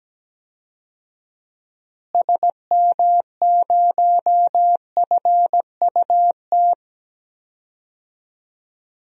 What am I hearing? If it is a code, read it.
Morse code, heard as a keyed tone, about 17 wpm, 700 Hz. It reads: SM0FUT
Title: DXCC-100 Call Signs - 6 Characters - Repeated 2x - 17wpm